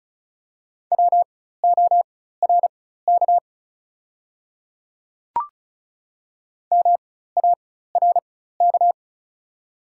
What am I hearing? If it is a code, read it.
Morse code, heard as a keyed tone, about 36 wpm, spaced as for 16 wpm, 700 Hz. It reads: WORK MARK
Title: 4x Character Spacing - Top 300 Words - 35wpm